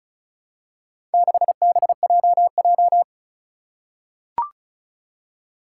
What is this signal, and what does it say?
Morse code, heard as a keyed tone, about 34 wpm, 700 Hz. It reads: 6BJJ
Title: Four Challenging Letter-Number Combinations 35wpm